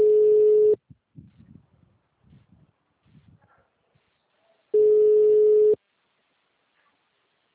Ringback tone (Europe, 425 Hz).
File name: out- -2016- - - .96269